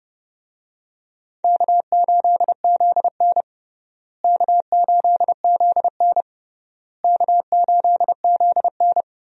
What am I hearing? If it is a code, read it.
Morse code, heard as a keyed tone, about 30 wpm, 700 Hz. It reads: K8ZN K8ZN K8ZN
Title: USA Call Signs - 4 Characters - Repeated 3x - 30wpm